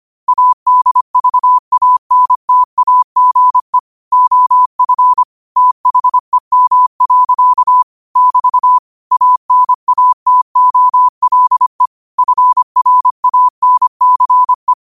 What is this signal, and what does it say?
Morse code, heard as a keyed tone, about 25 wpm, 1000 Hz. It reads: ADVANTAGE OF THEM. <BT> ANATOLE FRANCE
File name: qotdfiles / 25wpm